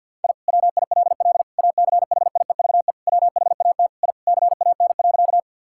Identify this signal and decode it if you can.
Morse code, heard as a keyed tone, about 50 wpm, 700 Hz. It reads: I WILL ACHIEVE WHAT I CAN.
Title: Sentences from Top 1000 Words - Review Part 6 - 50wpm